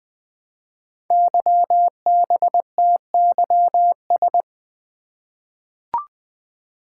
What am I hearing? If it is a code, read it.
Morse code, heard as a keyed tone, about 20 wpm, 700 Hz. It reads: YBTYS
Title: Code Groups - 5 Letters - 20wpm